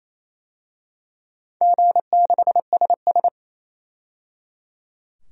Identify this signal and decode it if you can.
Morse code, heard as a keyed tone, about 28 wpm, 700 Hz. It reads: G6SS
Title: DXCC-100 Call Signs - 4 Characters - 28wpm